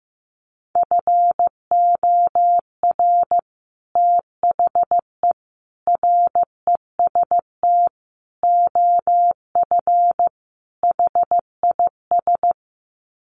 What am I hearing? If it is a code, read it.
Morse code, heard as a keyed tone, about 15 wpm, 700 Hz. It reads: FOR THE REST OF HIS
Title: Sets of 5 Words 15wpm